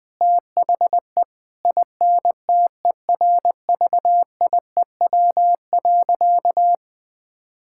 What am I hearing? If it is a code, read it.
Morse code, heard as a keyed tone, about 21 wpm, 700 Hz. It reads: THE INTERVIEW.
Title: Sentences from Top 1600 Words 20wpm